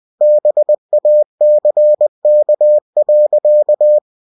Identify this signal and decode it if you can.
Morse code, heard as a keyed tone, about 20 wpm, 600 Hz. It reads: BACK.
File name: cw-books / winnie-the-pooh / winnie-20wpm-page-0069